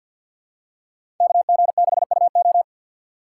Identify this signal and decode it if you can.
Morse code, heard as a keyed tone, about 52 wpm, 700 Hz. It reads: KG6UO
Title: USA Call Signs - 3x Character Spacing - 50wpm